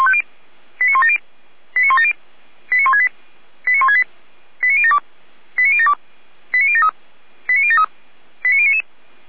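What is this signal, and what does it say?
Five-tone selective call (ZVEI), heard as quick runs of five tones, tones of about 70 ms. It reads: 89170 89170 89168 89168 89072 89072 89073 89073 89099
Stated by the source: Five-tone ZVEI selective call received by SDR.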